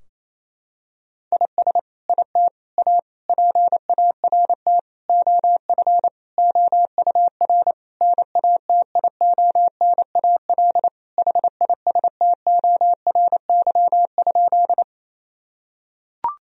Morse code, heard as a keyed tone, about 27 wpm, 700 Hz. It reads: IS IT A PART OF OUR NATIONAL HISTORY?
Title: Sentences from Top 1000 Words - Review Part 4 - 28wpm